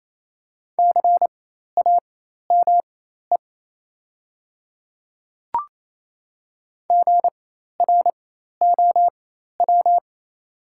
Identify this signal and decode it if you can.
Morse code, heard as a keyed tone, about 28 wpm, spaced as for 13 wpm, 690 Hz. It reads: CAME GROW
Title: 4x Character Spacing - Top 300 Words - 28wpm